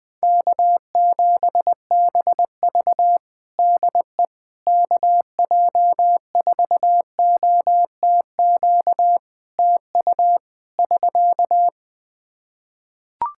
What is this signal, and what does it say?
Morse code, heard as a keyed tone, about 20 wpm, 700 Hz. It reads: K7BV DE KJ4OTQ TU <SK>
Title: Practice QSO - Exchange 4 - Ending - 20wpm